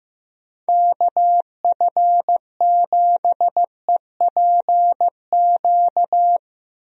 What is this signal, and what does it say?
Morse code, heard as a keyed tone, about 15 wpm, 700 Hz. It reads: KF7EPQ
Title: USA Call Signs - 6 Characters - Repeated 2x - 15wpm